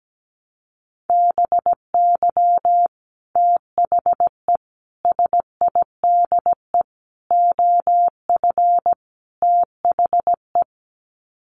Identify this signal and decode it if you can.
Morse code, heard as a keyed tone, about 17 wpm, 700 Hz. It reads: BY THE SIDE OF THE
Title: Sets of 5 Words Encore2 17wpm